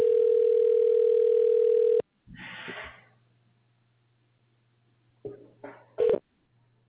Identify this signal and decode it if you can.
Ringback tone (North America), 440 + 480 Hz.